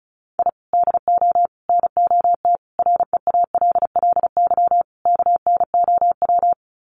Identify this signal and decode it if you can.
Morse code, heard as a keyed tone, about 34 wpm, 700 Hz. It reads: I DO NOT REALLY KNOW
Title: Sets of 5 Words Encore 35wpm